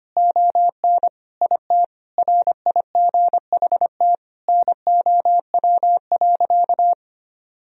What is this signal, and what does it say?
Morse code, heard as a keyed tone, about 25 wpm, 700 Hz. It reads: ON IT RIGHT NOW.
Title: Sentences from Top 1000 Words - Review Part 4 - 25wpm